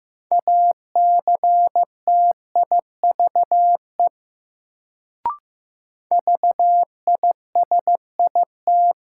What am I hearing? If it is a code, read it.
Morse code, heard as a keyed tone, about 15 wpm, 700 Hz. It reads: ACTIVE VISIT
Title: Sentences from Top 900 Words - New Word Drill - 15wpm